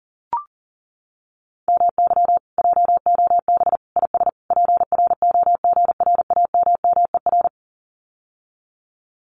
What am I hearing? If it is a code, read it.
Morse code, heard as a keyed tone, about 40 wpm, 700 Hz. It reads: MY JOB IS PROGRAMMER
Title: Common QSO Phrases Encore 40wpm